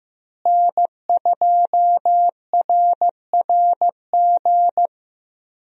Morse code, heard as a keyed tone, about 15 wpm, 700 Hz. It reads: N2RRG
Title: USA Call Signs - 5 Characters - Repeated 3x - 15wpm